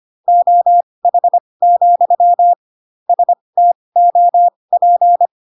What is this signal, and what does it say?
Morse code, heard as a keyed tone, about 25 wpm, spaced as for 20 wpm, 700 Hz. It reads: OH, STOP
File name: Pinocchio / Pinocchio ch1 15 25 ik1hge